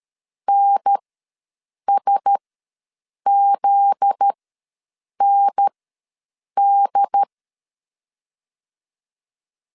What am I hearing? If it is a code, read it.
Morse code, heard as a keyed tone, about 13 wpm, spaced as for 7 wpm, 800 Hz. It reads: NSZND
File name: LEZIONI / 9 2 ws 12wpm 3406